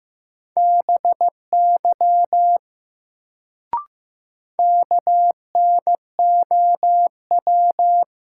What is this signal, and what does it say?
Morse code, heard as a keyed tone, about 15 wpm, 700 Hz. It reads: BY KNOW
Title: Sentences from Top 100 Words - New Word Drill - 15wpm